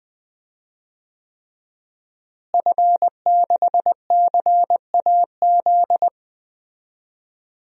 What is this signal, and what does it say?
Morse code, heard as a keyed tone, about 20 wpm, 700 Hz. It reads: F6CAZ
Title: DXCC-100 Call Signs - 5 Characters - Repeated 3x - 20wpm